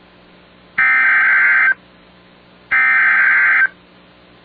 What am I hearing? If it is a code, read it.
Emergency Alert System (SAME) tones (header bursts), heard as harsh buzzing bursts. It is heard from 0.8 to 3.6 s.